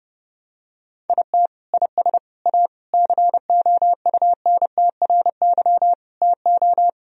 Morse code, heard as a keyed tone, about 32 wpm, 700 Hz. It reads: IT IS A COUNTRY TO
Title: Sentences from Top 1000 Words - Review Part 1 - 30wpm